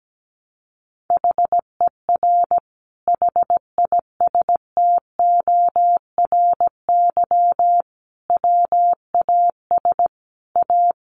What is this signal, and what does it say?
Morse code, heard as a keyed tone, about 17 wpm, 700 Hz. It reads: HER HISTORY WAS A
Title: Sentences from Top 1300 Words 17wpm